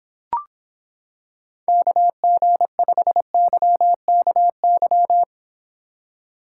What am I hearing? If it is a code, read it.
Morse code, heard as a keyed tone, about 26 wpm, 700 Hz. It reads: KG5YKY